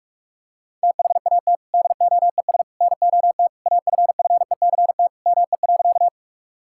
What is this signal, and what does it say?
Morse code, heard as a keyed tone, about 46 wpm, 700 Hz. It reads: THAT DOES NOT AFFECT ME.
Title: Sentences from Top 1000 Words - Review Part 6 - 45wpm